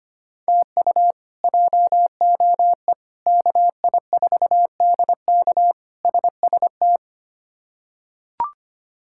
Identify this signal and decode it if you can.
Morse code, heard as a keyed tone, about 25 wpm, 700 Hz. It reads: TU JOE KI4DK SST
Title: K1USN SST QSO Practice - 25wpm